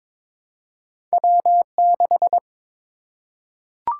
Morse code, heard as a keyed tone, about 22 wpm, 700 Hz. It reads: W6
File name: practice / 22wpm / Two Letter-Number Combinations - 22wpm